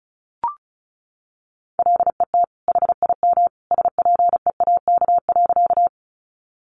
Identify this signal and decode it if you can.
Morse code, heard as a keyed tone, about 34 wpm, 700 Hz. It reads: LET HIM SPEAK.
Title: Sentences from Top 1100 Words 35wpm